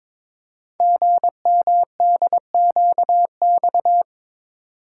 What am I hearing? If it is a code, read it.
Morse code, heard as a keyed tone, about 22 wpm, 700 Hz. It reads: GMDQX
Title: Code Groups - 5 Letters - 22wpm